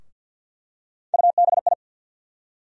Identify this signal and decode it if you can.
Morse code, heard as a keyed tone, about 51 wpm, 700 Hz. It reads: UBI
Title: USA Call Sign Postfixes - 3 Letters 50wpm